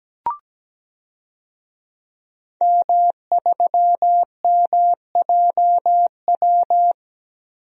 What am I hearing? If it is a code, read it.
Morse code, heard as a keyed tone, about 17 wpm, 700 Hz. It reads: M3MJW
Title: DXCC-100 Call Signs - 5 Characters - 17wpm